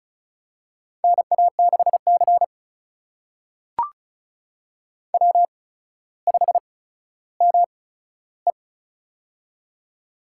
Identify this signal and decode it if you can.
Morse code, heard as a keyed tone, about 36 wpm, spaced as for 15 wpm, 700 Hz. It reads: NA6C W5ME
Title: USA Call Signs - 4 Characters - 8x Character Spacing - 35wpm